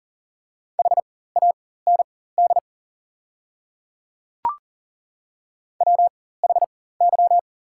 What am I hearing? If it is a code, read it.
Morse code, heard as a keyed tone, about 40 wpm, spaced as for 19 wpm, 700 Hz. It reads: HAND WHY
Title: 4x Character Spacing - Top 200 Words - 40wpm